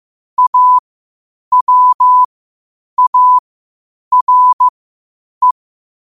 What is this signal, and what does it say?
Morse code, heard as a keyed tone, about 15 wpm, spaced as for 8 wpm, 1000 Hz. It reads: AWARE
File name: qotdfiles / 8wpm